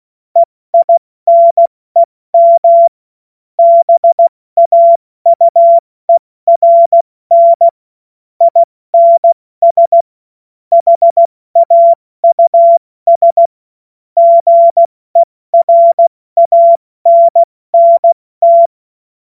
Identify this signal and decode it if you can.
Morse code, heard as a keyed tone, about 16 wpm, 680 Hz. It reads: EINEM BAUERN INS HAUS GERANNT
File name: Morsekurs / Witze / Witz 29 April 2025 70 F80